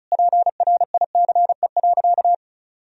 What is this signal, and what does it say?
Morse code, heard as a keyed tone, about 34 wpm, 700 Hz. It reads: PRICE.
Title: Sentences from Top 1000 Words - Review Part 4 - 35wpm